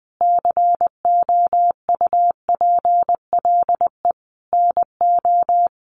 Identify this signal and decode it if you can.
Morse code, heard as a keyed tone, about 20 wpm, 700 Hz. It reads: COUPLE NO
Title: Sentences from Top 1100 Words 20wpm